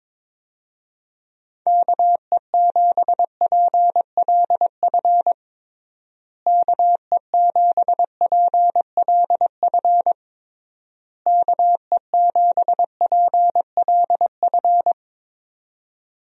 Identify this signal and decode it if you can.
Morse code, heard as a keyed tone, about 22 wpm, 700 Hz. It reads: KE7PLF KE7PLF KE7PLF
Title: USA Call Signs - 6 Characters - Repeated 3x - 22wpm